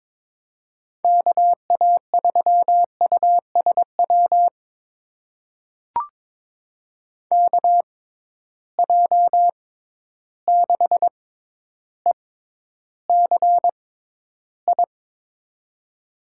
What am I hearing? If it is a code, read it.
Morse code, heard as a keyed tone, about 22 wpm, spaced as for 11 wpm, 700 Hz. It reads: KA3USW KJ6ECI
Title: USA Call Signs - 6 Characters - 6x Character Spacing - 22wpm